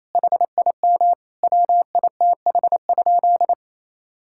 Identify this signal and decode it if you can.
Morse code, heard as a keyed tone, about 28 wpm, 700 Hz. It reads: HIM WITH?
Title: Sentences from Top 1000 Words - Review Part 5 - 28wpm